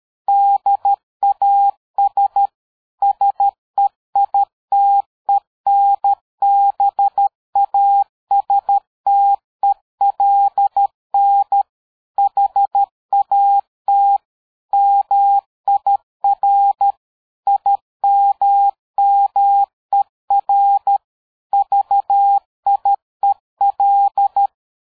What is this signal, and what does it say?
Morse code, heard as a keyed tone, about 13 wpm, 800 Hz. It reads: DAS SEITENBASTELN HAT MIR IMMER VIEL
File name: dk5ke / PersWorte 60